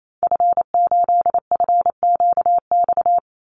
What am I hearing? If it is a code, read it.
Morse code, heard as a keyed tone, about 28 wpm, 700 Hz. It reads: F8FQX
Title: Call Signs Repeated 3x - 28wpm